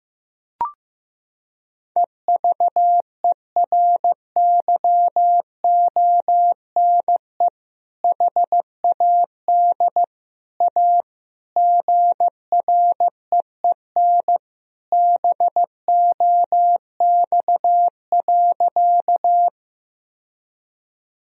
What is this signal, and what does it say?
Morse code, heard as a keyed tone, about 15 wpm, 700 Hz. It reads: EVERYONE HAD A GREEN BOX.
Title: Sentences from Top 1000 Words - Review Part 2 - 15wpm